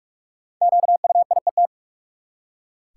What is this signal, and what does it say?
Morse code, heard as a keyed tone, about 43 wpm, 700 Hz. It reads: QUIET
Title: Five Letter Words - 45wpm